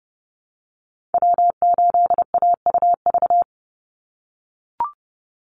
Morse code, heard as a keyed tone, about 29 wpm, 700 Hz. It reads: W8AUV
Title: USA Call Signs 30wpm